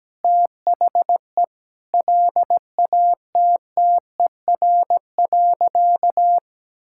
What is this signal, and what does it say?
Morse code, heard as a keyed tone, about 17 wpm, 700 Hz. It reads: THE LATTER.
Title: Sentences from Top 1600 Words 17wpm